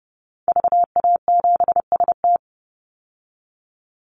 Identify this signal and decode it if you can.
Morse code, heard as a keyed tone, about 30 wpm, 700 Hz. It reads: VA7ST
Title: Call Signs Repeated 3x - 30wpm